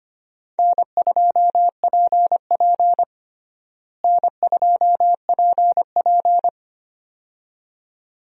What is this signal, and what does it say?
Morse code, heard as a keyed tone, about 25 wpm, 700 Hz. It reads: N2PP N2PP
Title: USA Call Signs - 4 Characters - Repeated 2x - 25wpm